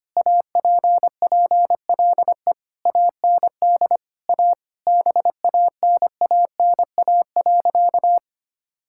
Morse code, heard as a keyed tone, about 25 wpm, 700 Hz. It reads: APPLE AND A BANANA.
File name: practice / 25wpm / Sentences with 2-Word Phrases - 25wpm